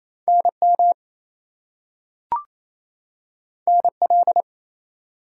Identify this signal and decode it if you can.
Morse code, heard as a keyed tone, about 28 wpm, 700 Hz. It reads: NM NL
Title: US States and Canadian Provinces 28wpm